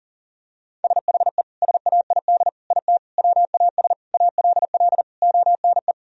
Morse code, heard as a keyed tone, about 40 wpm, 700 Hz. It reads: SHE SAID IT WAS ALL ONE
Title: Sentences from Top 1000 Words - Review Part 2 - 40wpm